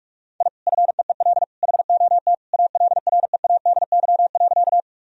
Morse code, heard as a keyed tone, about 46 wpm, 700 Hz. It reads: I FEEL HOT ALREADY.
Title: Sentences from Top 1000 Words - Review Part 3 - 45wpm